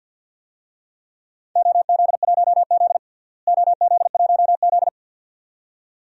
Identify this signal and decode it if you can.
Morse code, heard as a keyed tone, about 51 wpm, 700 Hz. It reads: OZ1Z OZ1Z
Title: DXCC-100 Call Signs - 4 Characters - Repeated 2x - 50wpm